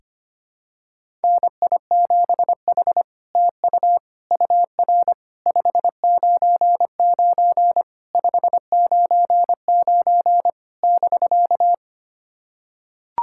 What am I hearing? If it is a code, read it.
Morse code, heard as a keyed tone, about 25 wpm, 700 Hz. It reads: NI7H TU UR 599 599 <BK>
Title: SOTA QSO Practice 25wpm